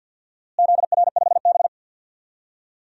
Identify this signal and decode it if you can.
Morse code, heard as a keyed tone, about 50 wpm, 700 Hz. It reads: ZR5B
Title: DXCC-100 Call Signs - 4 Characters - Repeated 3x - 50wpm